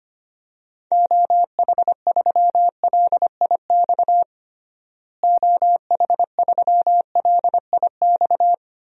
Morse code, heard as a keyed tone, about 25 wpm, 700 Hz. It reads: OH3LIX OH3LIX
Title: DXCC-100 Call Signs - 6 Characters - Repeated 2x - 25wpm